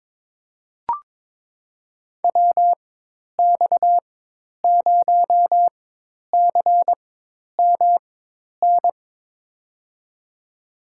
Morse code, heard as a keyed tone, about 22 wpm, spaced as for 10 wpm, 700 Hz. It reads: WX0CMN
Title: USA Call Signs - 4x Character Spacing - 22wpm